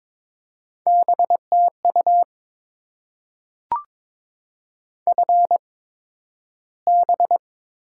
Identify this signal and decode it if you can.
Morse code, heard as a keyed tone, about 22 wpm, spaced as for 12 wpm, 700 Hz. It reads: BTU FB
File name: Top 20 QSO Elements - 8x Character Spacing - 22wpm